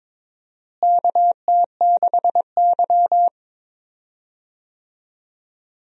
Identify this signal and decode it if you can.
Morse code, heard as a keyed tone, about 22 wpm, 700 Hz. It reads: KT6Y
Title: USA Call Signs - 4 Characters - Repeated 3x - 22wpm